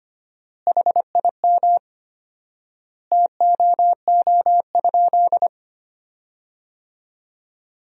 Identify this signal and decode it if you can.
Morse code, heard as a keyed tone, about 25 wpm, 700 Hz. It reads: HIM TOO?
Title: 4x Word-Spacing - Sentences from Top 500 Words - 25wpm